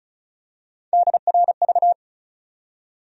Morse code, heard as a keyed tone, about 36 wpm, 700 Hz. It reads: DRV
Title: USA Call Sign Postfixes - 3 Letters 35wpm